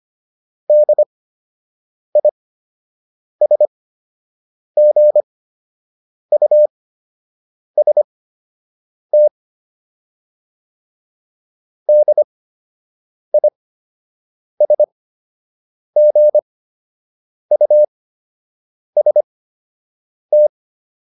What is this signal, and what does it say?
Morse code, heard as a keyed tone, about 25 wpm, spaced as for 7 wpm, 600 Hz. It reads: DISGUST DISGUST
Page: fundamental-practice-files